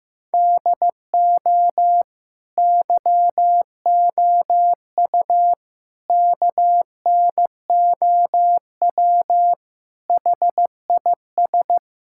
Morse code, heard as a keyed tone, about 15 wpm, 700 Hz. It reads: DO YOU KNOW HIS
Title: Sentences from Top 1000 Words - Review Part 6 - 15wpm